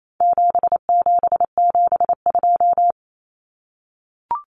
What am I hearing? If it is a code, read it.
Morse code, heard as a keyed tone, about 28 wpm, 700 Hz. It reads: 7772
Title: Numbers - 4-Digit 28wpm